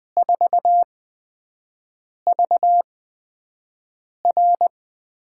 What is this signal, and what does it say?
Morse code, heard as a keyed tone, about 20 wpm, spaced as for 5 wpm, 700 Hz. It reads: 4V R
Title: USA Call Signs - 4 Characters - 8x Character Spacing - 20wpm